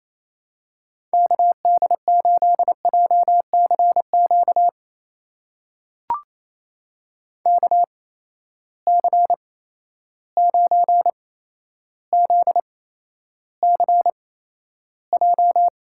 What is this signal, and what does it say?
Morse code, heard as a keyed tone, about 28 wpm, spaced as for 12 wpm, 700 Hz. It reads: KD8JCQ KC9ZCJ
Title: USA Call Signs - 6 Characters - 8x Character Spacing - 28wpm